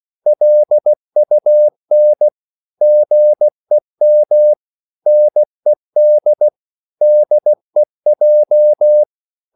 Morse code, heard as a keyed tone, about 16 wpm, 600 Hz. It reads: LUN GEM NED DEJ
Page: Korte danske ord 16wpm